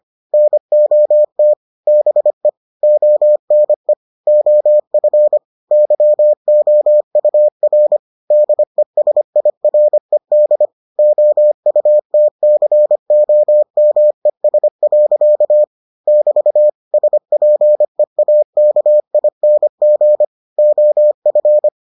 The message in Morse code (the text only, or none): NOT BE ONE OF YOUR DESIRED OUTCOMES. <BT> SPEAKING OF